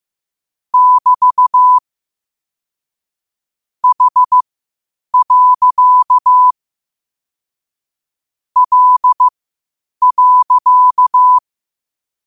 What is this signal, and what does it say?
Morse code, heard as a keyed tone, about 15 wpm, spaced as for 8 wpm, 1000 Hz. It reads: <BT> H. L.
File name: qotdfiles / 8wpm